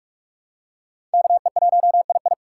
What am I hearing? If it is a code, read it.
Morse code, heard as a keyed tone, about 44 wpm, 700 Hz. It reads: KE1II
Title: USA Call Signs - 5 Characters - 2x Character Spacing - 45wpm